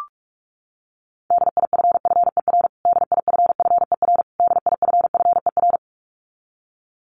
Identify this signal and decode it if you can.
Morse code, heard as a keyed tone, about 44 wpm, 700 Hz. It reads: DIFFER DIFFER DIFFER
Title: Top 200 Words Repeated 45wpm